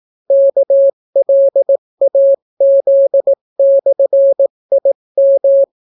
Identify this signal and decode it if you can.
Morse code, heard as a keyed tone, about 18 wpm, 550 Hz. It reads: KLAZ/IM